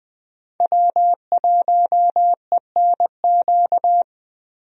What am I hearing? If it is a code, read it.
Morse code, heard as a keyed tone, about 20 wpm, 700 Hz. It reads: W1ENQ
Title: USA Call Signs - 5 Characters - Repeated 3x - 20wpm